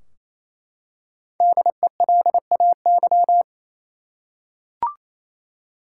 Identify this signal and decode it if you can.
Morse code, heard as a keyed tone, about 28 wpm, 700 Hz. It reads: DELAY
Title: Five Letter Words - 28wpm